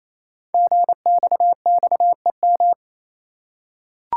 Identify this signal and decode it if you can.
Morse code, heard as a keyed tone, about 27 wpm, 700 Hz. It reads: GXXEM
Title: Code Groups - 5 Letters - 28wpm